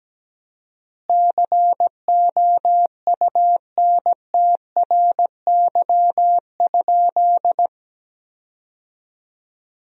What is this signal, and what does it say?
Morse code, heard as a keyed tone, about 17 wpm, 700 Hz. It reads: COUNTRY?
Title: 4x Word-Spacing - Sentences from Top 300 Words - 17wpm